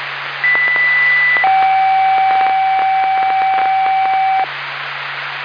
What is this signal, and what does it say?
Two-tone page, heard as a one-second tone, then a three-second tone, about 2000 then 770 Hz, with noise. It is heard from 0.4 to 4.4 s.